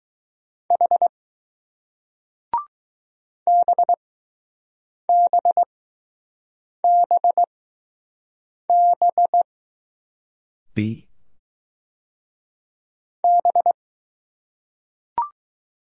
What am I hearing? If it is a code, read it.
Morse code, heard as a keyed tone, about 20 wpm, spaced as for 6 wpm, 700 Hz. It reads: H BBBB B